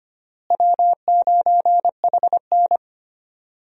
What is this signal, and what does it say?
Morse code, heard as a keyed tone, about 25 wpm, 700 Hz. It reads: W9HN
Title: USA Call Signs - 4 Characters - Repeated 3x - 25wpm